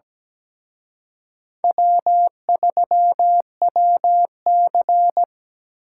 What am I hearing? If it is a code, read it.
Morse code, heard as a keyed tone, about 20 wpm, 700 Hz. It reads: W3WC
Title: Speed Racing - USA 1x2 Call Signs 17wpm